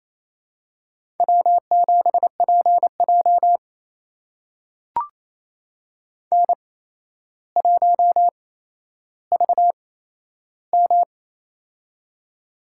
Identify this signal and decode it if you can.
Morse code, heard as a keyed tone, about 28 wpm, spaced as for 12 wpm, 700 Hz. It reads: W7PJ N1VM
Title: USA Call Signs - 4 Characters - 8x Character Spacing - 28wpm